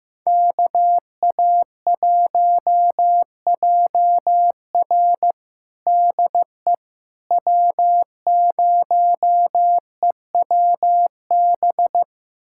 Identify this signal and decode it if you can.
Morse code, heard as a keyed tone, about 15 wpm, 700 Hz. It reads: KA1JR DE W0EWB
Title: USA Callsign DE Callsign - 5 Characters - 15wpm